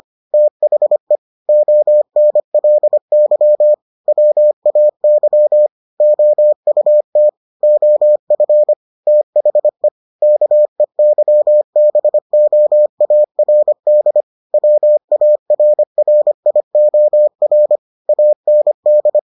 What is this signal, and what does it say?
Morse code, heard as a keyed tone, about 25 wpm, 600 Hz. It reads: THE ONLY WAY OUT OF THE KEYBOARD WARRIOR AND